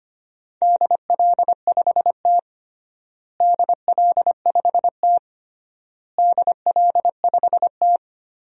Morse code, heard as a keyed tone, about 25 wpm, 700 Hz. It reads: DL5T DL5T DL5T
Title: DXCC-100 Call Signs - 4 Characters - Repeated 3x - 25wpm